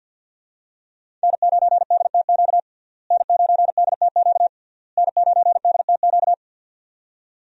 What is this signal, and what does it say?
Morse code, heard as a keyed tone, about 48 wpm, 700 Hz. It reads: N9DTQ N9DTQ N9DTQ
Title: USA Call Signs - 5 Characters - Repeated 3x - 50wpm